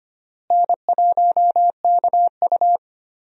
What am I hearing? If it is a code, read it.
Morse code, heard as a keyed tone, about 25 wpm, 700 Hz. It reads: N1KU
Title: USA Call Signs - 4 Characters - Repeated 2x - 25wpm